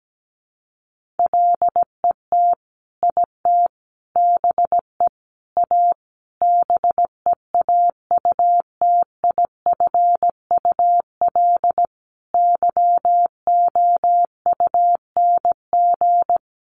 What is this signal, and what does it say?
Morse code, heard as a keyed tone, about 17 wpm, 700 Hz. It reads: LET IT BE A BEAUTIFUL YOUNG
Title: Sentences from Top 1100 Words 17wpm